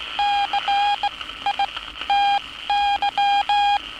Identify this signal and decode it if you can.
Morse code, heard as a keyed tone, about 16 wpm, spaced as for 13 wpm, 810 Hz. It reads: CITY
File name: training / NAVPERS 11049to11060 / NAVPERS 11059 RA RB